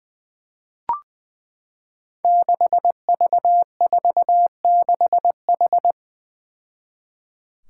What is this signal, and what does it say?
Morse code, heard as a keyed tone, about 20 wpm, 700 Hz. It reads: 6V46H
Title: Five Challenging Letter-Number Combinations 20wpm